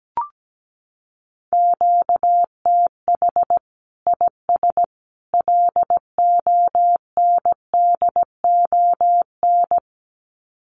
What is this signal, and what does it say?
Morse code, heard as a keyed tone, about 17 wpm, 700 Hz. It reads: QTH IS LONDON
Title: Common QSO Phrases 17wpm